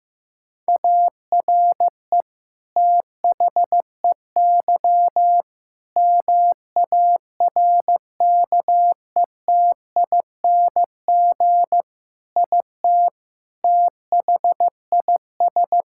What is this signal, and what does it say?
Morse code, heard as a keyed tone, about 16 wpm, 700 Hz. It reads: ARE THEY MARKETING IT THIS
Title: Sentences from Top 1000 Words - Review Part 5 - 15wpm